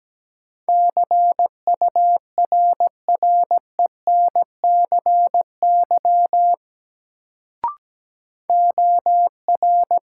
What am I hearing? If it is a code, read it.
Morse code, heard as a keyed tone, about 17 wpm, 700 Hz. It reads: CURRENCY OR
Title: Sentences from Top 1500 Words - New Word Drill - 17wpm